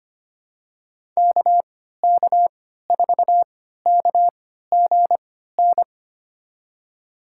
Morse code, heard as a keyed tone, about 25 wpm, spaced as for 14 wpm, 700 Hz. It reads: KK4KGN
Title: USA Call Signs - 6 Characters - 3x Character Spacing - 25wpm